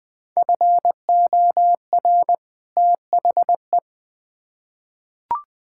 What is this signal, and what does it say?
Morse code, heard as a keyed tone, about 20 wpm, 700 Hz. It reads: FOR THE
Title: Build A Phrase - 2-Word Phrases - 20wpm